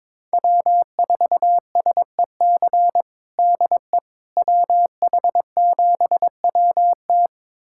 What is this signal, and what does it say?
Morse code, heard as a keyed tone, about 22 wpm, 700 Hz. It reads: W4SEC DE WH7WT
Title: USA Callsign DE Callsign - 5 Characters - 22wpm